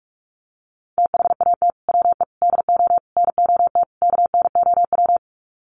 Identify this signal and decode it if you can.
Morse code, heard as a keyed tone, about 46 wpm, 700 Hz. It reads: THAT WE DO NOT KNOW
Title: Sets of 5 Words Encore2 45wpm